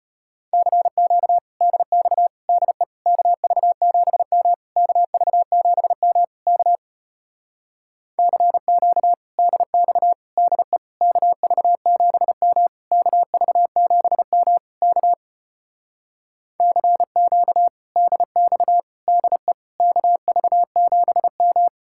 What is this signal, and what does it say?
Morse code, heard as a keyed tone, about 33 wpm, 700 Hz. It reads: CQ DX DE KV7M KV7M K CQ DX DE KV7M KV7M K CQ DX DE KV7M